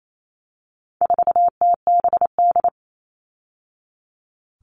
Morse code, heard as a keyed tone, about 28 wpm, 700 Hz. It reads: 4TBD
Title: All Cut Numbers - 4-Digit 28wpm